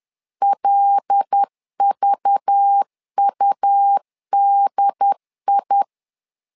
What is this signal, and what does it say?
Morse code, heard as a keyed tone, about 10 wpm, 800 Hz. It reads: LVUDI
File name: LEZIONI / 5 LBVC / 5 LBV 10wpm 2248